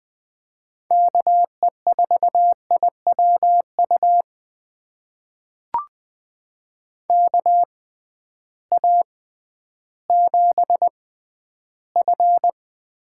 Morse code, heard as a keyed tone, about 20 wpm, spaced as for 12 wpm, 700 Hz. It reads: KE4IWU KA7F
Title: USA Call Signs - 6x Character Spacing - 20wpm